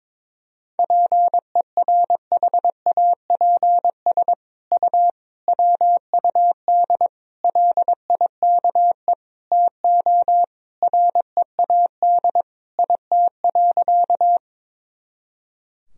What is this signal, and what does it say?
Morse code, heard as a keyed tone, about 22 wpm, 700 Hz. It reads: PERHAPS U WUD LIKE TO READ IT.
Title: Sentences from Top 500 Words and Abbreviations 22wpm